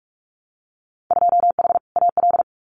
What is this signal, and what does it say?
Morse code, heard as a keyed tone, about 46 wpm, 700 Hz. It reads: 2H AL
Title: Winter Field Day Exchanges 45wpm